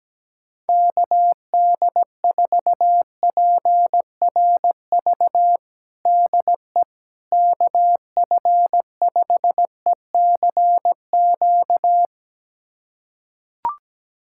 Morse code, heard as a keyed tone, about 17 wpm, 700 Hz. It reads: KD4PRV DE KF5ECQ
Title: USA Callsign DE Callsign - 6 Characters - 17wpm